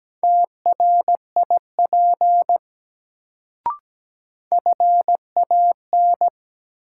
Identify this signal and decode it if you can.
Morse code, heard as a keyed tone, about 17 wpm, 700 Hz. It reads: TRIP FAN
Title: Sentences from Top 1300 Words - New Word Drill Alt - 17wpm